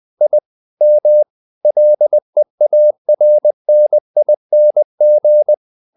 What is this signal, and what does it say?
Morse code, heard as a keyed tone, about 20 wpm, 600 Hz. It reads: I M LEARNING
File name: the old man and the sea part 2 20wpm